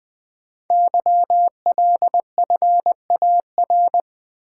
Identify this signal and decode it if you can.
Morse code, heard as a keyed tone, about 20 wpm, 700 Hz. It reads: YLFAR
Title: Code Groups - 5 Letters - 20wpm